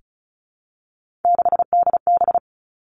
Morse code, heard as a keyed tone, about 36 wpm, 700 Hz. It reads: 6DB
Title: Three Challenging Letter-Number Combinations 35wpm